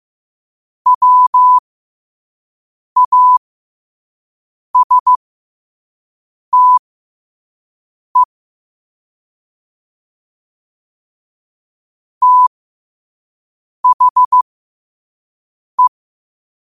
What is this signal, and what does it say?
Morse code, heard as a keyed tone, about 15 wpm, spaced as for 5 wpm, 1000 Hz. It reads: WASTE THE
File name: qotdfiles / 5wpm